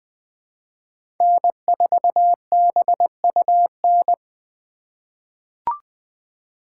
Morse code, heard as a keyed tone, about 20 wpm, 700 Hz. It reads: N4BUN
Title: 2023 POTA Hunter Call Signs - 20wpm